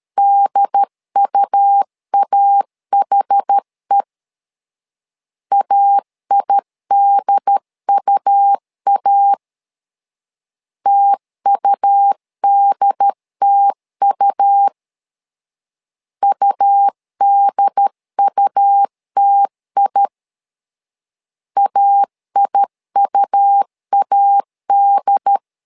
Morse code, heard as a keyed tone, about 13 wpm, 800 Hz. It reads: DUAHE AIDUA TUDTU UDUTI AIUAD
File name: LEZIONI / 2 UHDN / 2 UHD 12wpm 1540